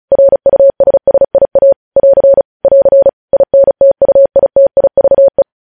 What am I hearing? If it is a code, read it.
Morse code, heard as a keyed tone, about 36 wpm, 550 Hz. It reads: RUSSIA <AR> <AR> INTUITIVE
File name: podcast / News headlines-1x-35x35